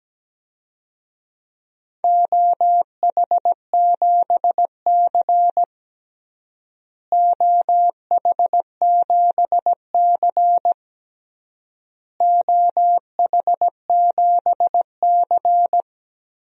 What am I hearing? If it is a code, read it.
Morse code, heard as a keyed tone, about 17 wpm, 700 Hz. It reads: OH7C OH7C OH7C
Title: DXCC-100 Call Signs - 4 Characters - Repeated 3x - 17wpm